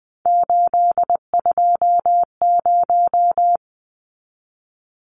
Morse code, heard as a keyed tone, about 20 wpm, 700 Hz. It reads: 820
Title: Numbers - 3-Digit 20wpm